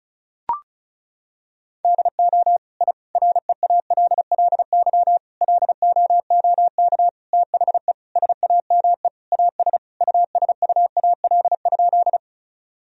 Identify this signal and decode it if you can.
Morse code, heard as a keyed tone, about 36 wpm, 700 Hz. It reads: DO I REALLY LOOK THE SAME AS USUAL?
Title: Sentences from Top 1000 Words - Review Part 2 - 35wpm